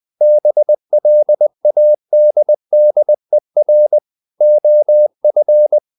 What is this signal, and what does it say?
Morse code, heard as a keyed tone, about 20 wpm, 600 Hz. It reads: BLADDER OF
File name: the old man and the sea / the old man and the sea part 1 20wpm